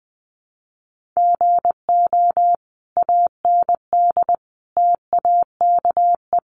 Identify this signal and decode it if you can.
Morse code, heard as a keyed tone, about 20 wpm, 700 Hz. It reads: GO AND TAKE
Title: Sentences from Top 1400 Words 20wpm